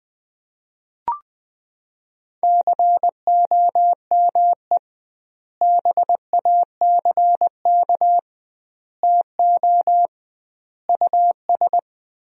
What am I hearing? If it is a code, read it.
Morse code, heard as a keyed tone, about 21 wpm, 700 Hz. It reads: COME BACK TO US
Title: 2x Word-Spacing - Sentences from Top 500 Words - 20wpm